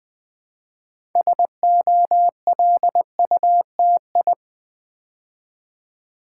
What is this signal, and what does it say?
Morse code, heard as a keyed tone, about 20 wpm, 700 Hz. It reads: SOLUTI
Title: Build A Word - 2 Letters At A Time - 8-Letter Words - 20wpm